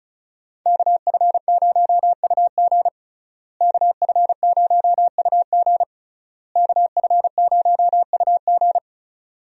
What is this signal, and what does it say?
Morse code, heard as a keyed tone, about 34 wpm, 700 Hz. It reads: KF0UG KF0UG KF0UG
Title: USA Call Signs - 5 Characters - Repeated 3x - 35wpm